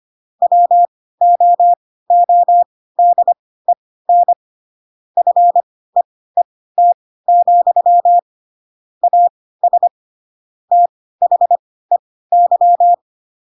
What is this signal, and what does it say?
Morse code, heard as a keyed tone, about 25 wpm, spaced as for 16 wpm, 700 Hz. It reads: WOODEN FEET, AS THEY
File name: Pinocchio / Pinocchio ch3 10 25 ik1hge